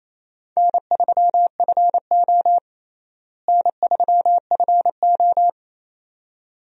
Morse code, heard as a keyed tone, about 28 wpm, 700 Hz. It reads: N3FO N3FO
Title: USA Call Signs - 4 Characters - Repeated 2x - 28wpm